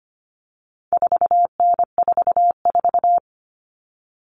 Morse code, heard as a keyed tone, about 25 wpm, 700 Hz. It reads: 4N44
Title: All Cut Numbers - 4-Digit 25wpm